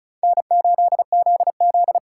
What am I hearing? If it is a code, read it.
Morse code, heard as a keyed tone, about 40 wpm, 700 Hz. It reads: N8ZZ